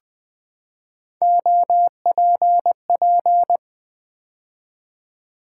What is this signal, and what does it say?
Morse code, heard as a keyed tone, about 20 wpm, 700 Hz. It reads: OPP
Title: Build A Word - 3 Letters At A Time - 8-Letter Words - 20wpm